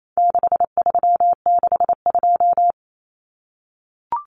Morse code, heard as a keyed tone, about 28 wpm, 700 Hz. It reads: 6362
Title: Numbers - 4-Digit 28wpm